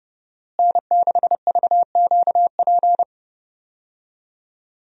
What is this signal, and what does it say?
Morse code, heard as a keyed tone, about 32 wpm, 700 Hz. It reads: N6VQP
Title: USA Call Signs - 5 Characters - Repeated 3x - 30wpm